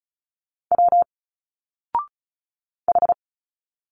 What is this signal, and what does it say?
Morse code, heard as a keyed tone, about 35 wpm, 700 Hz. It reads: W H
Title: Single Letters 35wpm